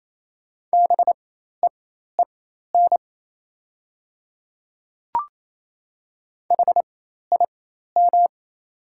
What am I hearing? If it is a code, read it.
Morse code, heard as a keyed tone, about 27 wpm, spaced as for 13 wpm, 690 Hz. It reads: BEEN HIM
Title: 4x Character Spacing - Top 200 Words - 28wpm